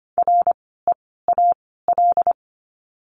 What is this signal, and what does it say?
Morse code, heard as a keyed tone, about 25 wpm, 700 Hz. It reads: R E A L
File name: course / 25:10wpm / 094 Lesson 16f - Letter c - Words 25wpm